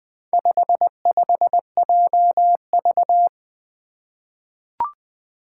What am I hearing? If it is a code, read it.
Morse code, heard as a keyed tone, about 20 wpm, 700 Hz. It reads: 55JV
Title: Four Challenging Letter-Number Combinations 20wpm